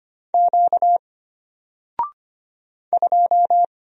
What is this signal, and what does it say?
Morse code, heard as a keyed tone, about 25 wpm, 700 Hz. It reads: Q 2
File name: Single-Letter-Number-25wpm